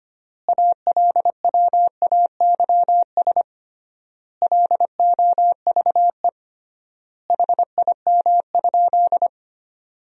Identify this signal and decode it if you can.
Morse code, heard as a keyed tone, about 25 wpm, 700 Hz. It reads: ALWAYS LOVE HIM?
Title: 3x Word-Spacing - Sentences from Top 400 Words - 25wpm